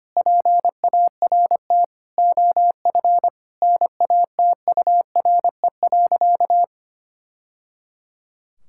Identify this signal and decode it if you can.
Morse code, heard as a keyed tone, about 25 wpm, 700 Hz. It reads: PART OF NATURE.
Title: Sentences from Top 1000 Words - Review Part 6 - 25wpm